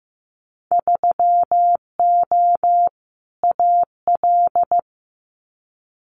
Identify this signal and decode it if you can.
Morse code, heard as a keyed tone, about 15 wpm, 700 Hz. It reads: 3O AL
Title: Winter Field Day Exchanges Encore 15wpm